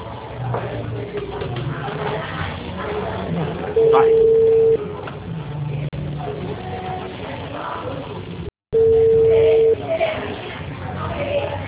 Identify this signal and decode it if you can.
Ringback tone (China, 450 Hz), with speech.
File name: out- -2008- - - .20189